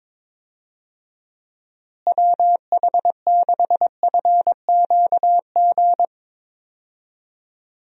Morse code, heard as a keyed tone, about 22 wpm, 700 Hz. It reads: WH6FQG
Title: USA Call Signs - 6 Characters - Repeated 3x - 22wpm